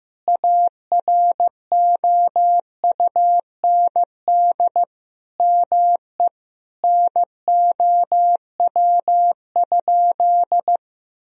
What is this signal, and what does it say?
Morse code, heard as a keyed tone, about 16 wpm, 700 Hz. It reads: AROUND ME NOW?
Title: Sentences from Top 1000 Words - Review Part 3 - 15wpm